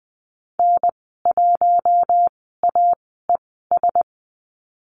Morse code, heard as a keyed tone, about 20 wpm, spaced as for 14 wpm, 700 Hz. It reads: N1AES
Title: 042 Lesson 8g - Number 1 - Call Signs 20wpm